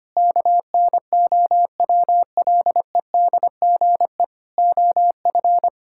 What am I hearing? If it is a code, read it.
Morse code, heard as a keyed tone, about 25 wpm, 700 Hz. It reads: KNOWLEDGE OF
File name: practice / 25wpm / Sentences with 9-Word Phrases - 25wpm